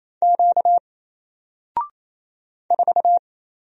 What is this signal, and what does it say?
Morse code, heard as a keyed tone, about 27 wpm, 700 Hz. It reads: Q 4
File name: Single-Letter-Number-28wpm